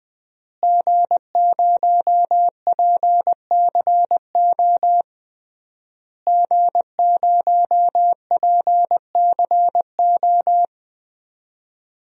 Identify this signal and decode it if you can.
Morse code, heard as a keyed tone, about 20 wpm, 700 Hz. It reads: G0PCO G0PCO
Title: DXCC-100 Call Signs - 5 Characters - Repeated 3x - 20wpm